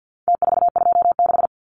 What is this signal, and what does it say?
Morse code, heard as a keyed tone, about 50 wpm, 700 Hz. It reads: T426
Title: Numbers - Contest Style - 4-Digit 50wpm